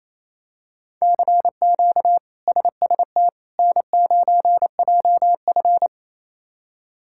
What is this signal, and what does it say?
Morse code, heard as a keyed tone, about 28 wpm, 700 Hz. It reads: CQ SST N9JF
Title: K1USN SST QSO Practice - 28wpm